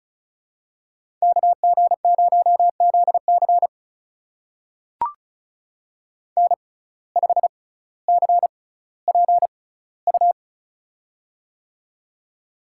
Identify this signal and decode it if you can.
Morse code, heard as a keyed tone, about 35 wpm, spaced as for 18 wpm, 700 Hz. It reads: KG0ZC N5CPU
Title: USA Call Signs - 5 Characters - 6x Character Spacing - 35wpm